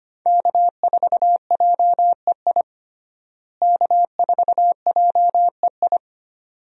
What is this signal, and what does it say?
Morse code, heard as a keyed tone, about 25 wpm, 700 Hz. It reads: K4JEI K4JEI
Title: USA Call Signs - 5 Characters - Repeated 2x - 25wpm